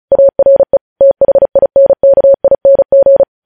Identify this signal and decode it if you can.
Morse code, heard as a keyed tone, about 36 wpm, 550 Hz. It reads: ARE THINKING
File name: podcast / News headlines-1x-35x35